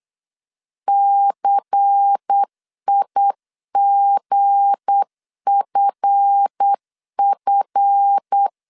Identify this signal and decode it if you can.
Morse code, heard as a keyed tone, about 8 wpm, 800 Hz. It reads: CIGFF
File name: LEZIONI / 7 GF9Z / 7 GF 8wpm 3108